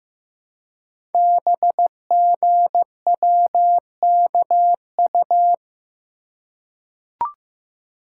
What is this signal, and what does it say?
Morse code, heard as a keyed tone, about 15 wpm, 700 Hz. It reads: BGWKU
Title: Code Groups - 5 Letters - 15wpm